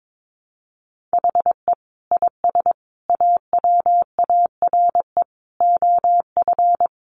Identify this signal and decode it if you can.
Morse code, heard as a keyed tone, about 22 wpm, 700 Hz. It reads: HE IS AWARE OF
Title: Sentences from Top 1200 Words 22wpm